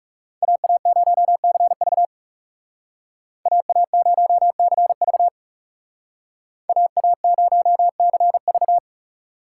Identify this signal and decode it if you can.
Morse code, heard as a keyed tone, about 36 wpm, 700 Hz. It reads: AA0CV AA0CV AA0CV
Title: Speed Racing - USA 2x2 Call Signs 30wpm